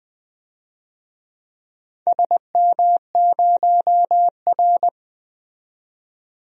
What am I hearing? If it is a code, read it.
Morse code, heard as a keyed tone, about 20 wpm, 700 Hz. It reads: SM0R
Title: DXCC-100 Call Signs - 4 Characters - 20wpm